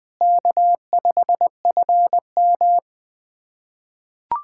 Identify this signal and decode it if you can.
Morse code, heard as a keyed tone, about 20 wpm, 700 Hz. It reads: K5FM
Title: USA Callsign DE Callsign - 4 Characters - 20wpm